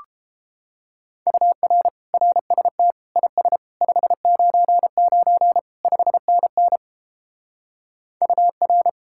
Morse code, heard as a keyed tone, about 29 wpm, 700 Hz. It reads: UR RST IS 599 5NN UR